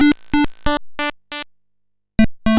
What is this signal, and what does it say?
There is no code.